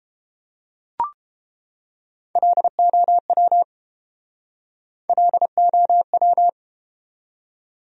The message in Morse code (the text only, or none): LOW LOW